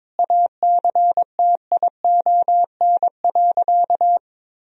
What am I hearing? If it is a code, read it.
Morse code, heard as a keyed tone, about 22 wpm, 700 Hz. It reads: ACTION.
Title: Sentences from Top 1000 Words - Review Part 4 - 22wpm